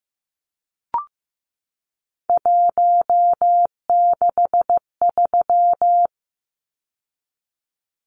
Morse code, heard as a keyed tone, about 15 wpm, 700 Hz. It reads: 163
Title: Numbers - 3-Digit 15wpm